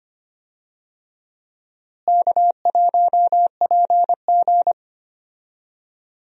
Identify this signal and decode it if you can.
Morse code, heard as a keyed tone, about 25 wpm, 700 Hz. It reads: K1PG
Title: USA Call Signs - 4 Characters - Repeated 3x - 25wpm